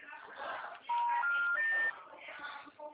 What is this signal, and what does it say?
Special information tone, with speech.
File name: out- -2010- - - .76029